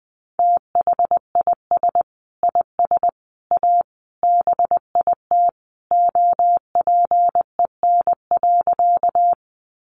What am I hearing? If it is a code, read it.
Morse code, heard as a keyed tone, about 20 wpm, 700 Hz. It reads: THIS IS A BIT OPEN.
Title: Sentences from Top 900 Words 20wpm